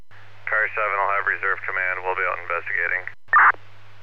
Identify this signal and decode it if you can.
Emergency Alert System (SAME) tones, heard as harsh buzzing bursts, with speech. It is heard from 0.4 to 3.1 s.